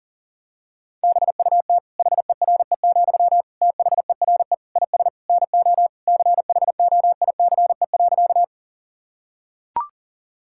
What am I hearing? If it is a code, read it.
Morse code, heard as a keyed tone, about 40 wpm, 700 Hz. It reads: BUT HERE, THERE IS NO CHOICE.
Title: Sentences from Top 1000 Words - Review Part 6 - 40wpm